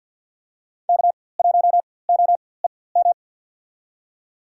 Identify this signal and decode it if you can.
Morse code, heard as a keyed tone, about 52 wpm, spaced as for 23 wpm, 700 Hz. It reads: K1OEM
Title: USA Call Signs - 4x Character Spacing - 50wpm